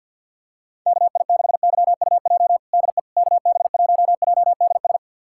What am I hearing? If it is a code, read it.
Morse code, heard as a keyed tone, about 48 wpm, 700 Hz. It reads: KI6YUJ DE KB1JDS
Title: USA Callsign DE Callsign 50wpm